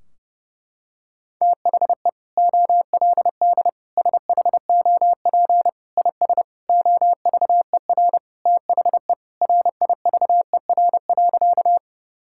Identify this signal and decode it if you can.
Morse code, heard as a keyed tone, about 32 wpm, 700 Hz. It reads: THE OLD SHOP IS OVER THE RIVER.
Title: Sentences from Top 1600 Words 30wpm